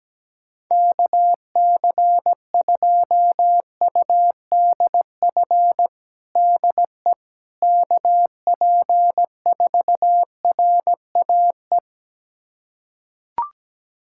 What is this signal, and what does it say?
Morse code, heard as a keyed tone, about 17 wpm, 700 Hz. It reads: KC2UDF DE KP4RAE
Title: USA Callsign DE Callsign - 6 Characters - 17wpm